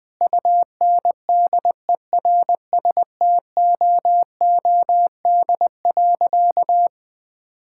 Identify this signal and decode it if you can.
Morse code, heard as a keyed tone, about 21 wpm, 700 Hz. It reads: UNDERSTOOD.
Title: Sentences with 5-Word Phrases - 20wpm